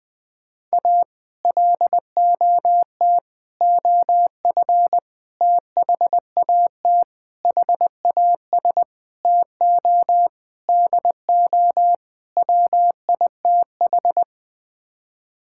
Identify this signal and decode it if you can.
Morse code, heard as a keyed tone, about 20 wpm, 700 Hz. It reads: A LOT OF THAT HAS TO DO WITH
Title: Sets of 8 Words - 20wpm